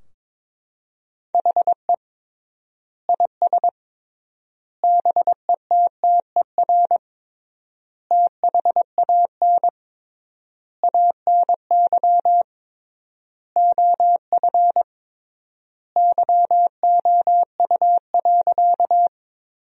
Morse code, heard as a keyed tone, about 22 wpm, 700 Hz. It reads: HE IS BETTER THAN ANY OF YOU.
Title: 3x Word-Spacing - Sentences from Top 400 Words - 22wpm